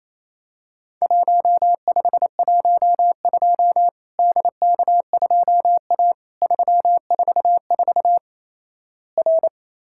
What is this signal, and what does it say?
Morse code, heard as a keyed tone, about 28 wpm, 700 Hz. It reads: 1512 DK2A 344 R
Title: Worked All Europe DX Contest - QTC Exchange - 28wpm